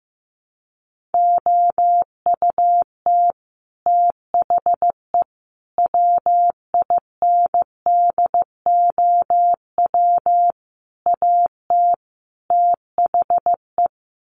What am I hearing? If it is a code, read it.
Morse code, heard as a keyed tone, about 15 wpm, 700 Hz. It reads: OUT THE WINDOW AT THE
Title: Sets of 5 Words Encore2 15wpm